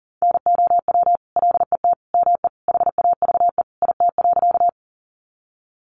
Morse code, heard as a keyed tone, about 40 wpm, 700 Hz. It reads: NOW LET ME HAVE IT.
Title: Sentences from Top 300 Words 40wpm